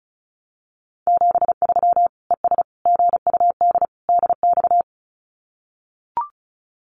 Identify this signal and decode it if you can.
Morse code, heard as a keyed tone, about 34 wpm, 700 Hz. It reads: 73 ES GUD DX
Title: Common QSO Phrases 35wpm